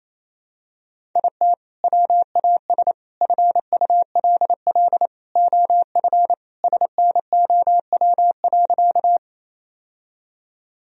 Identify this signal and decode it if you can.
Morse code, heard as a keyed tone, about 28 wpm, 700 Hz. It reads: IT WAS FULL OF SNOW.
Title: Sentences from Top 1000 Words - Review Part 3 - 28wpm